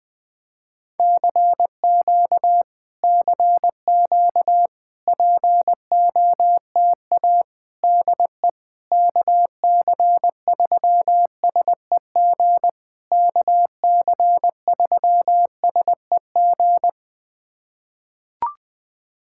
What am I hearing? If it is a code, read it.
Morse code, heard as a keyed tone, about 20 wpm, 690 Hz. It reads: CQ CQ POTA DE KC3SEG KC3SEG
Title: POTA QSO Practice - 20wpm